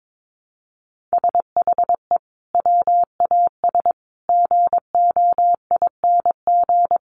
Morse code, heard as a keyed tone, about 22 wpm, 700 Hz. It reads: SHE WAS GOING
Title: Sets of 4 Words 22wpm